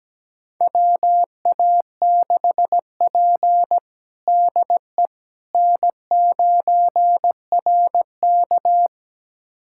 Morse code, heard as a keyed tone, about 17 wpm, 700 Hz. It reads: WA6P DE N9RK
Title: USA Callsign DE Callsign - 4 Characters - 17wpm